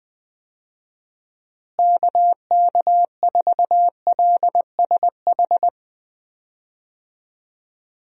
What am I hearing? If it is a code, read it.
Morse code, heard as a keyed tone, about 21 wpm, 700 Hz. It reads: KK4LSH
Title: USA Call Signs - 6 Characters - Repeated 3x - 20wpm